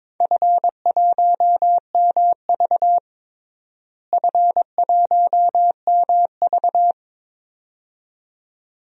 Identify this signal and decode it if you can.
Morse code, heard as a keyed tone, about 22 wpm, 700 Hz. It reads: F1MV F1MV
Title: DXCC-100 Call Signs - 4 Characters - Repeated 2x - 22wpm